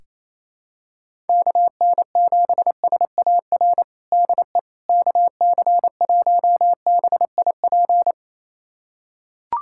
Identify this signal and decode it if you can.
Morse code, heard as a keyed tone, about 28 wpm, 700 Hz. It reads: KN7SAR DE KC1BIP
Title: USA Callsign DE Callsign 28wpm